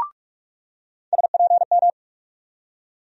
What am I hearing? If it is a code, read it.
Morse code, heard as a keyed tone, about 47 wpm, 700 Hz. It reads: SPM
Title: USA Call Sign Postfixes - 3 Letters 45wpm